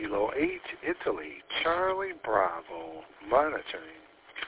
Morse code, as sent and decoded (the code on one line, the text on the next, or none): none